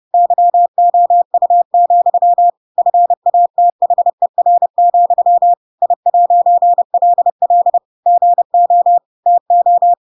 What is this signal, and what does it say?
Morse code, heard as a keyed tone, about 30 wpm, 700 Hz. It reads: YOU, FATHER, I'LL GO TO
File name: Pinocchio / Pinocchio ch8 30 30 ik1hge